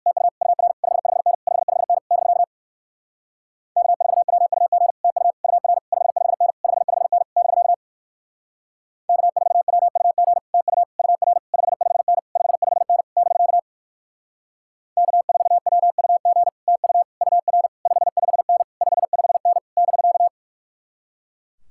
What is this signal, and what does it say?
Morse code, heard as a keyed tone, about 56 wpm, 700 Hz. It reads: TU UR 55N 55N <BK> K4WUG TU UR 55N 55N <BK> K4WUG TU UR 55N 55N <BK> K4WUG TU UR 55N 55N <BK>
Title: Speed-Racing Common QSO Phrases - 45wpm